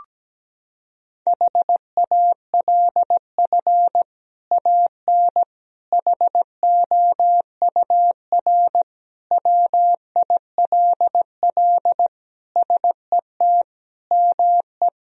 Morse code, heard as a keyed tone, about 17 wpm, 700 Hz. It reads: HALF AN HOUR WILL SET ME
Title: Sentences from Top 1000 Words - Review Part 1 - 17wpm